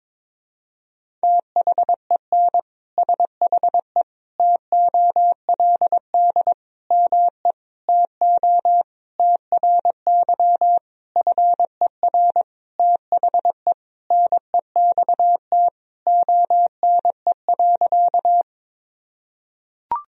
Morse code, heard as a keyed tone, about 22 wpm, 700 Hz. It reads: THEN SHE TOLD ME TO TRY FER THE NEXT ONE.
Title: Sentences from Top 400 Words and Abbreviations 22wpm